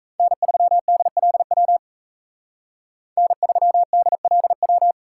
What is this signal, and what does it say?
Morse code, heard as a keyed tone, about 36 wpm, 700 Hz. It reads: N3DLW N3DLW